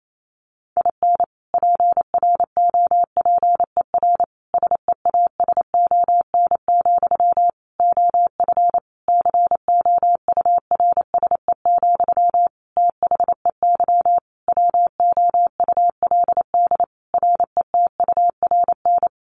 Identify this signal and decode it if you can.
Morse code, heard as a keyed tone, about 28 wpm, 700 Hz. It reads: IN PROPER SEASON, OF COURSE, THEY WOULD RETURN
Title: Sentences from Top 1400 Words 28wpm